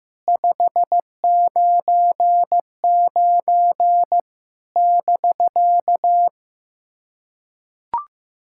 Morse code, heard as a keyed tone, about 15 wpm, 700 Hz. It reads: 599 <BK>
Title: SOTA QSO Practice 15wpm